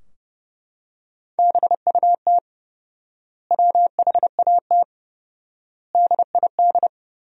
Morse code, heard as a keyed tone, about 32 wpm, 700 Hz. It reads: BUT WHAT DID
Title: 4x Word-Spacing - Sentences from Top 200 Words - 30wpm